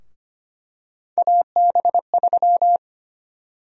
Morse code, heard as a keyed tone, about 25 wpm, 700 Hz. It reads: AB3
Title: USA Call Sign Prefixes - Repeated 2x - 25wpm